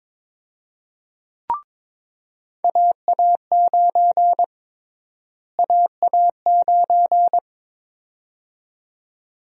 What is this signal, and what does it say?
Morse code, heard as a keyed tone, about 22 wpm, 700 Hz. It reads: AA9 AA9
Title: USA Call Sign Prefixes - Repeated 2x - 22wpm